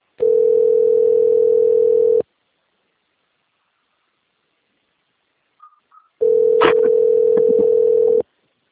Ringback tone (North America), 440 + 480 Hz.